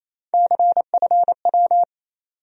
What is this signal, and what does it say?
Morse code, heard as a keyed tone, about 28 wpm, 700 Hz. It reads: CFW
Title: USA Call Sign Postfixes - 3 Letters 28wpm